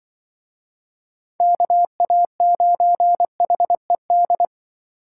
Morse code, heard as a keyed tone, about 24 wpm, 700 Hz. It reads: KA9HED
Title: Speed Racing - USA 2x3 Call Signs 20wpm